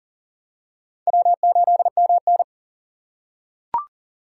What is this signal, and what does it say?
Morse code, heard as a keyed tone, about 40 wpm, 700 Hz. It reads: W8MN
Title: USA Call Signs - 4 Characters - 2x Character Spacing - 40wpm